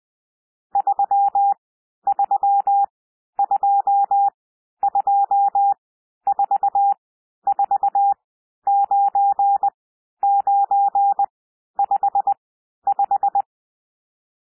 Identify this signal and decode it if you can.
Morse code, heard as a keyed tone, about 20 wpm, spaced as for 11 wpm, 800 Hz. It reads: 3322449955
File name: cwmp3 / N1 SOUND NUMBERS 20 wpm